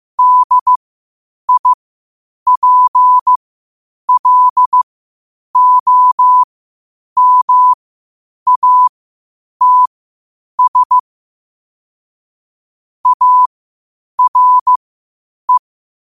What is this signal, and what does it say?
Morse code, heard as a keyed tone, about 15 wpm, spaced as for 8 wpm, 1000 Hz. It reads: DIPLOMATS ARE
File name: qotdfiles / 8wpm